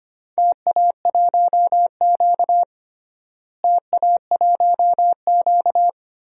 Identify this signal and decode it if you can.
Morse code, heard as a keyed tone, about 25 wpm, 700 Hz. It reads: TA1Q TA1Q
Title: DXCC-100 Call Signs - 4 Characters - Repeated 2x - 25wpm